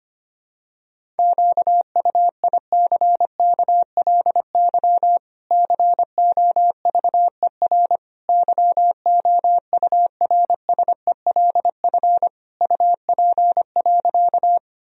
Morse code, heard as a keyed tone, about 25 wpm, 700 Hz. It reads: QUICKLY COVER YOURSELF UP.
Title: Sentences from Top 1000 Words - Review Part 1 - 25wpm